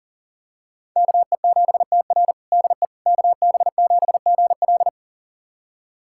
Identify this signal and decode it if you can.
Morse code, heard as a keyed tone, about 37 wpm, 700 Hz. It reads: KE7TR DE KB7GL
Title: USA Callsign DE Callsign - 5 Characters - 40wpm